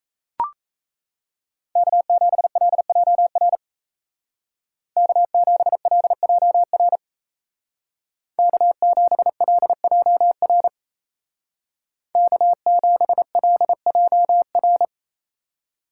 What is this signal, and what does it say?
Morse code, heard as a keyed tone, about 34 wpm, 700 Hz. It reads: K7LJR K7LJR K7LJR K7LJR
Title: Speed Racing - USA 1x3 Call Signs 28wpm